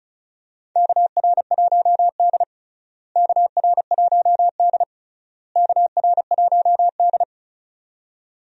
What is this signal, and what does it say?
Morse code, heard as a keyed tone, about 36 wpm, 700 Hz. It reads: KR1D KR1D KR1D
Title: USA Call Signs - 4 Characters - Repeated 3x - 35wpm